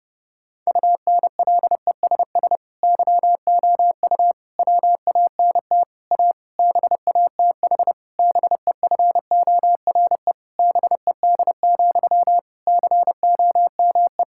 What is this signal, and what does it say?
Morse code, heard as a keyed tone, about 32 wpm, 700 Hz. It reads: UNLESS YOU WANT A BATH BEFORE BED, COME
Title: Sentences from Top 1600 Words 30wpm